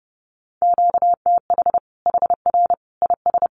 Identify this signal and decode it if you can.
Morse code, heard as a keyed tone, about 30 wpm, 700 Hz. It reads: QTH HR IS
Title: Common QSO Phrases 30wpm